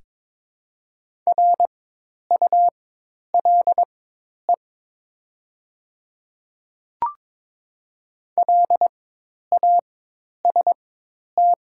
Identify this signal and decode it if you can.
Morse code, heard as a keyed tone, about 22 wpm, spaced as for 10 wpm, 700 Hz. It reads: RULE LAST
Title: 4x Character Spacing - Top 500 Words - 22wpm